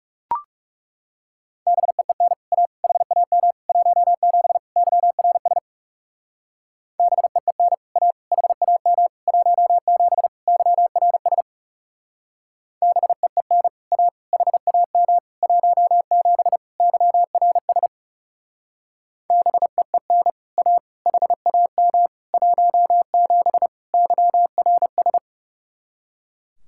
Morse code, heard as a keyed tone, about 37 wpm, 700 Hz. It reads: BEEN A HAM 17 YRS BEEN A HAM 17 YRS BEEN A HAM 17 YRS BEEN A HAM 17 YRS
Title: Speed-Racing Common QSO Phrases - 30wpm